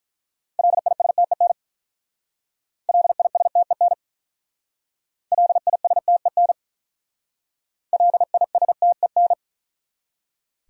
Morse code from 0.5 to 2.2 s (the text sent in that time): LISTEN